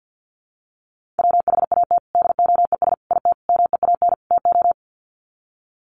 Morse code, heard as a keyed tone, about 50 wpm, 700 Hz. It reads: WHAT DOES IT MEAN TO
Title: Sets of 5 Words Encore2 50wpm